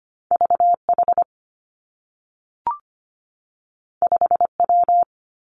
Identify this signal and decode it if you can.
Morse code, heard as a keyed tone, about 25 wpm, 700 Hz. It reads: V H 5 W
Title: Two Challenging Letter-Number Combinations 25wpm